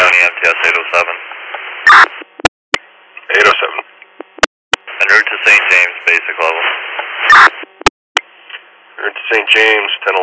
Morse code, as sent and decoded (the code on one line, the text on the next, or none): none